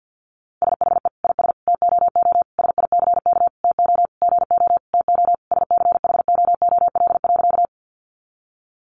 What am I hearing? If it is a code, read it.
Morse code, heard as a keyed tone, about 50 wpm, 700 Hz. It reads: SHE IS TOO SICK TO GO TO SCHOOL.
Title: Easy Phrases 50wpm